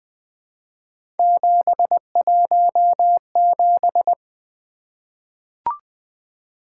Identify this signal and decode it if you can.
Morse code, heard as a keyed tone, about 20 wpm, 700 Hz. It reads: 717
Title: Three Letter-Number Combinations - 20wpm